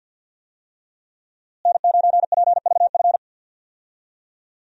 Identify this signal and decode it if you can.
Morse code, heard as a keyed tone, about 46 wpm, 700 Hz. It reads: N9PVF
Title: USA Call Signs - 5 Characters - Repeated 2x - 50wpm